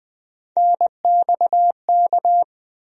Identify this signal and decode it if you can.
Morse code, heard as a keyed tone, about 21 wpm, 700 Hz. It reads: NXK
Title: USA Call Sign Postfixes - 3 Letters 20wpm